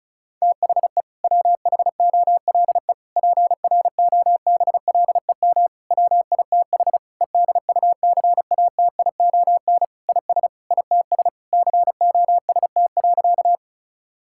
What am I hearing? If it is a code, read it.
Morse code, heard as a keyed tone, about 34 wpm, 700 Hz. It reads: THE WHOLE PROBLEM WITH EDUCATION IS ITS COST.
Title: Sentences from Top 1000 Words - Review Part 6 - 35wpm